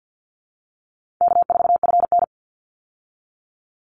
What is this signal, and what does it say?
Morse code, heard as a keyed tone, about 48 wpm, 700 Hz. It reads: K4FN
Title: Easy Call Signs Repeated 3x - 50wpm